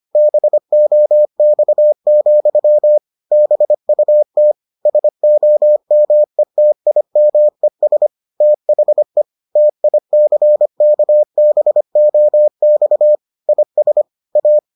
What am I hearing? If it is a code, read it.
Morse code, heard as a keyed tone, about 25 wpm, 600 Hz. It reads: BOX, BUT SOMETIMES THE TICKBOX IS A